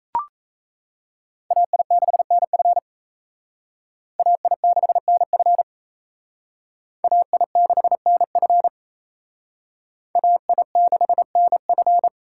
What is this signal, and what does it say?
Morse code, heard as a keyed tone, about 36 wpm, 700 Hz. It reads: AI6NF AI6NF AI6NF AI6NF